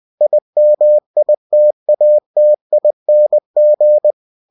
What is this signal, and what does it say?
Morse code, heard as a keyed tone, about 20 wpm, 600 Hz. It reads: IMITATING
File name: cw-books / winnie-the-pooh / winnie-20wpm-page-0072